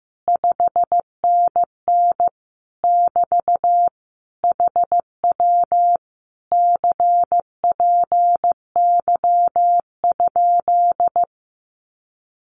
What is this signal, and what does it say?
Morse code, heard as a keyed tone, about 15 wpm, 700 Hz. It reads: 5NN <BT> HW CPY?
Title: Common QSO Phrases Encore 15wpm